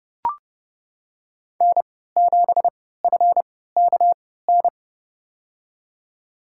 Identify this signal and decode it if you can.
Morse code, heard as a keyed tone, about 30 wpm, spaced as for 17 wpm, 700 Hz. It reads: N7FKN
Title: USA Call Signs - 5 Characters - 3x Character Spacing - 30wpm